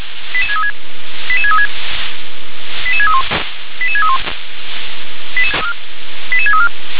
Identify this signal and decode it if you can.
Five-tone selective call (ZVEI), heard as quick runs of five tones, tones of about 70 ms. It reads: 88636 88636 88631 88631 88634